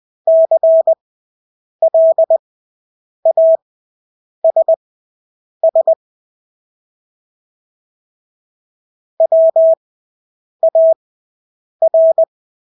Morse code, heard as a keyed tone, about 20 wpm, spaced as for 8 wpm, 650 Hz. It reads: CLASS WAR
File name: Engelske ord 20/8 wpm